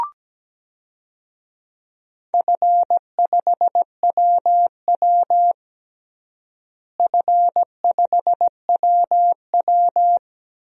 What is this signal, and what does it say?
Morse code, heard as a keyed tone, about 17 wpm, 700 Hz. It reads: F5WW F5WW
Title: DXCC-100 Call Signs - 4 Characters - Repeated 3x - 17wpm